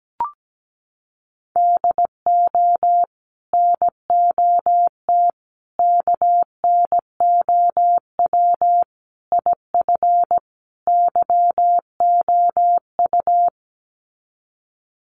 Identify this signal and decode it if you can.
Morse code, heard as a keyed tone, about 17 wpm, 700 Hz. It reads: DO NOT KNOW IF YOU
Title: Sets of 5 Words 17wpm